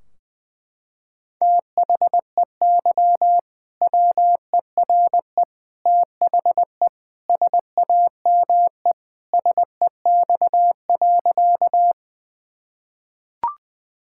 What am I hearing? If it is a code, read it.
Morse code, heard as a keyed tone, about 21 wpm, 700 Hz. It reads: THEY WERE THE SAME SEX.
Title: Sentences from Top 1600 Words 20wpm